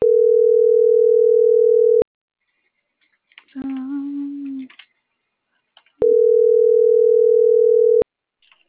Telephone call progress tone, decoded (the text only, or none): ringback tone